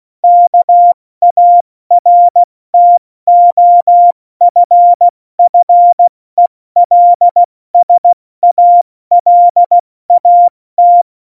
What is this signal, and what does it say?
Morse code, heard as a keyed tone, about 16 wpm, 700 Hz. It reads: KARTOFFELSALAT
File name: Morsekurs / Witze / Witz 03. November 2020, 70 F80